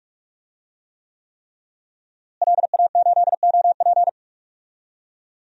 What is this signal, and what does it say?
Morse code, heard as a keyed tone, about 46 wpm, 700 Hz. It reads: LA8OP
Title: DXCC-100 Call Signs - 5 Characters - Repeated 3x - 45wpm